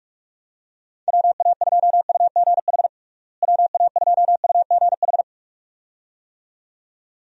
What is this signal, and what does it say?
Morse code, heard as a keyed tone, about 41 wpm, 700 Hz. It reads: WA2UGH WA2UGH
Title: USA Call Signs - 6 Characters - Repeated 2x - 45wpm